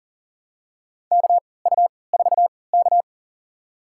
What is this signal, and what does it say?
Morse code, heard as a keyed tone, about 40 wpm, spaced as for 23 wpm, 700 Hz. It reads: KU4K
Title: USA Call Signs - 4 Characters - 3x Character Spacing - 40wpm